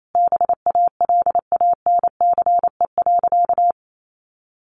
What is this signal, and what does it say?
Morse code, heard as a keyed tone, about 28 wpm, 700 Hz. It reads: BALANCE.
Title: Sentences from Top 1200 Words 28wpm